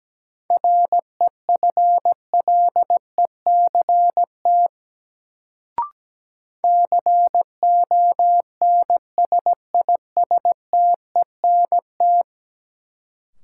Morse code, heard as a keyed tone, about 17 wpm, 700 Hz. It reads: REFLECT CONSISTENT
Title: Sentences from Top 1500 Words - New Word Drill Alt - 17wpm